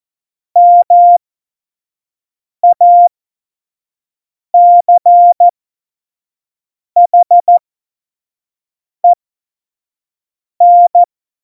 Morse code, heard as a keyed tone, about 14 wpm, spaced as for 5 wpm, 700 Hz. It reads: MACHEN
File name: Morsekurs / Witze / Witz 15. Januar 2019, 25 F70